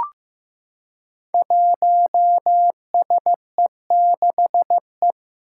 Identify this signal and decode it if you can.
Morse code, heard as a keyed tone, about 15 wpm, 700 Hz. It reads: 1SE6E
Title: Five Challenging Letter-Number Combinations 15wpm